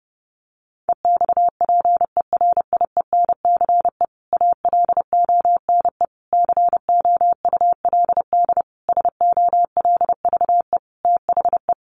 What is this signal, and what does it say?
Morse code, heard as a keyed tone, about 28 wpm, 700 Hz. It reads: EXPERIENCE ALONE COULD SOLVE THE
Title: Sentences from Top 1000 Words 30wpm